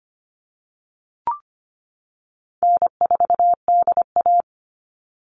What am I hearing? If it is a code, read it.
Morse code, heard as a keyed tone, about 25 wpm, 700 Hz. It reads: N4DA
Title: NA SOTA Chaser Callsigns 25wpm